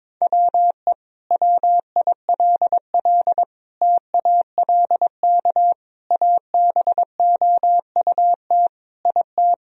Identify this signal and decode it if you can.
Morse code, heard as a keyed tone, about 22 wpm, 700 Hz. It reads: WE WILL TALK ABOUT IT
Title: Sentences from Top 1000 Words - Review Part 4 - 22wpm